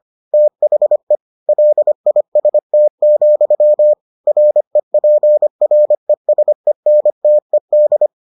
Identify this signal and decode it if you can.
Morse code, heard as a keyed tone, about 25 wpm, 600 Hz. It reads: THE LIST, REPRESENTED